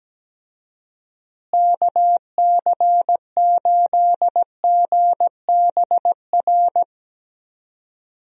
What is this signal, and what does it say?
Morse code, heard as a keyed tone, about 17 wpm, 700 Hz. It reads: KC8GBR
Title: USA Call Signs - 6 Characters - Repeated 3x - 17wpm